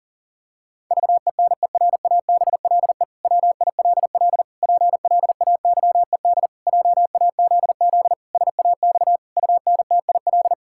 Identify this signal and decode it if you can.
Morse code, heard as a keyed tone, about 40 wpm, 700 Hz. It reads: VENERABLE WILL PLAYED JAZZ SAX UNTIL
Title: Pangram Challenge v2 - 40wpm